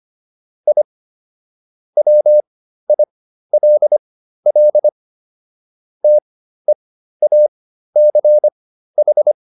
Morse code, heard as a keyed tone, about 25 wpm, spaced as for 13 wpm, 600 Hz. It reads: I WILL TEACH